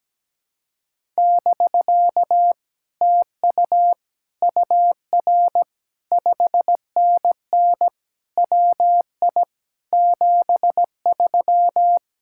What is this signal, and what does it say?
Morse code, heard as a keyed tone, about 17 wpm, 700 Hz. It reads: <BK> TU UR 5NN WI 73
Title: Speed-Racing Common QSO Phrases - 17wpm